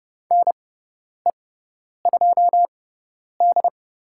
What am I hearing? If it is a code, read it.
Morse code, heard as a keyed tone, about 30 wpm, spaced as for 10 wpm, 700 Hz. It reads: NE2D
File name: Lesson 16g - Letter c - Call Signs 30wpm